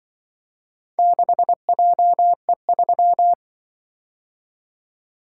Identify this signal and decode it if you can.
Morse code, heard as a keyed tone, about 25 wpm, 700 Hz. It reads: 6JE3
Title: Speed-Racing - Four Letter-Number Combinations - 20wpm